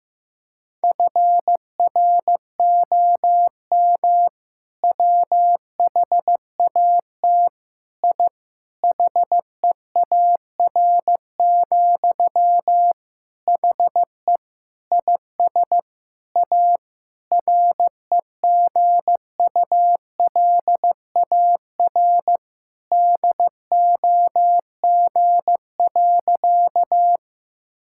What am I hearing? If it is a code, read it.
Morse code, heard as a keyed tone, about 15 wpm, 700 Hz. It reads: FROM WHAT I HEAR, HE IS A REGULAR DOG.
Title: Sentences from Top 1000 Words - Review Part 6 - 15wpm